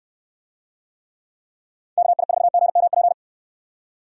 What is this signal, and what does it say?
Morse code, heard as a keyed tone, about 65 wpm, 700 Hz. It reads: KI4GMP